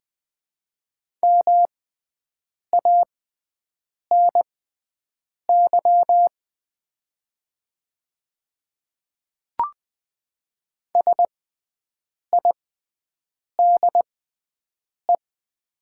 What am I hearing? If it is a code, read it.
Morse code, heard as a keyed tone, about 20 wpm, spaced as for 7 wpm, 700 Hz. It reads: MANY SIDE
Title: 6x Character Spacing - Top 100 Words - 20wpm